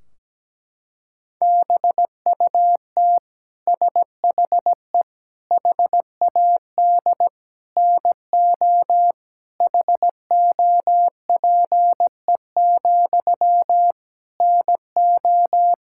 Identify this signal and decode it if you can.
Morse code, heard as a keyed tone, about 17 wpm, 700 Hz. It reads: BUT SHE HAD NO HOPE, NO
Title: Sentences from Top 1600 Words 17wpm